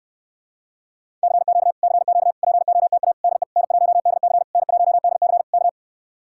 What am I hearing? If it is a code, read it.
Morse code, heard as a keyed tone, about 72 wpm, 700 Hz. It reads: CQ CQ POTA DE N1DC N1DC K